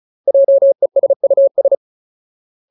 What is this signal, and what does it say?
Morse code, heard as a keyed tone, about 34 wpm, 550 Hz. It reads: JESUS
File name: Ultimate Word List / 35WPM Ultimate